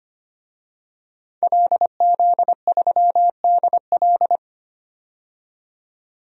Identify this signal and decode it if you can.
Morse code, heard as a keyed tone, about 25 wpm, 690 Hz. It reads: LZ3DL
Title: DXCC-100 Call Signs - 5 Characters - 25wpm